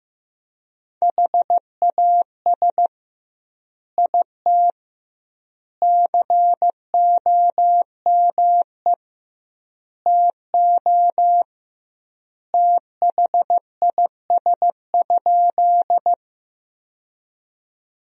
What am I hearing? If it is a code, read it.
Morse code, heard as a keyed tone, about 15 wpm, 700 Hz. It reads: HAS IT COME TO THIS?
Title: 2x Word-Spacing - Sentences from Top 100 Words - 15wpm